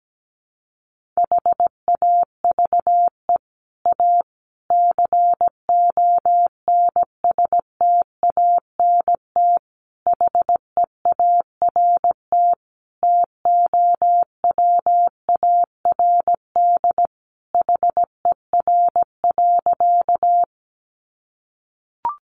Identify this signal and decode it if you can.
Morse code, heard as a keyed tone, about 17 wpm, 700 Hz. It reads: HAVE A CONSTANT HEART TOWARD HER.
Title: Sentences from Top 1500 Words 17wpm